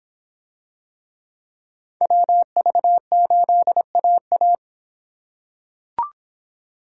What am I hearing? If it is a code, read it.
Morse code, heard as a keyed tone, about 25 wpm, 700 Hz. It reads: WV8AA